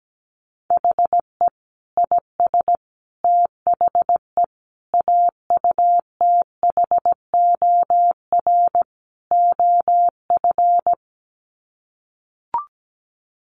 Morse code, heard as a keyed tone, about 17 wpm, 700 Hz. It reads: HE IS THE AUTHOR OF
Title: Sets of 5 Words Encore2 17wpm